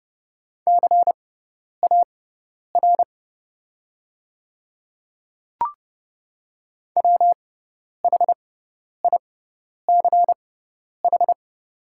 Morse code, heard as a keyed tone, about 30 wpm, spaced as for 10 wpm, 700 Hz. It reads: CAR WHICH
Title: 6x Character Spacing - Top 300 Words - 30wpm